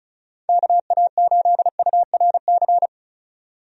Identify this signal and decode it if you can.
Morse code, heard as a keyed tone, about 34 wpm, 700 Hz. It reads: KA8URC
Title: USA Call Signs - 6 Characters - 35wpm